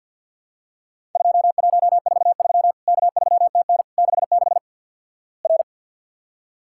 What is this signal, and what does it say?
Morse code, heard as a keyed tone, about 51 wpm, 700 Hz. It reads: 2143 K3TN 66 R
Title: Worked All Europe DX Contest - QTC Exchange - 50wpm